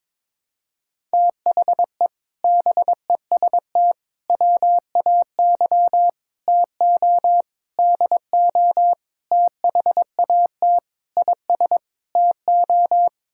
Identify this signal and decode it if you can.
Morse code, heard as a keyed tone, about 22 wpm, 700 Hz. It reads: THE BEST WAY TO DO THAT IS TO
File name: practice / 22wpm / Sets of 8 Words - 22wpm